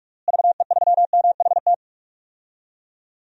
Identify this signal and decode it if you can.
Morse code, heard as a keyed tone, about 42 wpm, 700 Hz. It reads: VE3MHT
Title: DXCC-100 Call Signs - 6 Characters - Repeated 3x - 45wpm